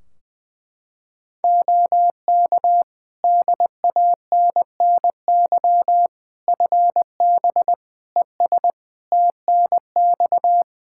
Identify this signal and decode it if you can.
Morse code, heard as a keyed tone, about 21 wpm, 700 Hz. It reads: OK DANNY FB ES TNX
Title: Practice QSO - Exchange 2 - Info - 20wpm